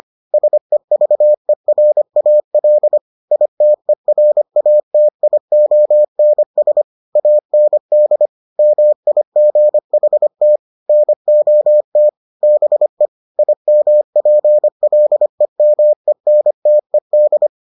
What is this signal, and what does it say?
Morse code, heard as a keyed tone, about 25 wpm, 600 Hz. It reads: SEVERAL ITERATIONS AND MIGHT NOT BE IMPLEMENTED